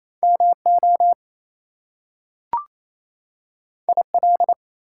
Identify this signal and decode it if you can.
Morse code, heard as a keyed tone, about 28 wpm, 700 Hz. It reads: MO IL
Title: US States and Canadian Provinces 28wpm